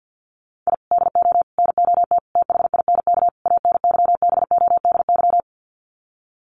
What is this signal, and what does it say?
Morse code, heard as a keyed tone, about 48 wpm, 700 Hz. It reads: I DO NOT THINK ANYBODY
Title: Sets of 5 Words 50wpm